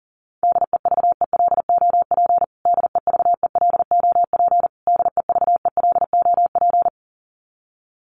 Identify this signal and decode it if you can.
Morse code, heard as a keyed tone, about 40 wpm, 700 Hz. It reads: DEVELOP DEVELOP DEVELOP
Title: Top 500 Words Repeated 40wpm